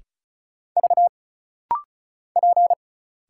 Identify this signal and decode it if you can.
Morse code, heard as a keyed tone, about 36 wpm, 700 Hz. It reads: V P
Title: Single Letter-Number Alt - Rapid-Fire 35wpm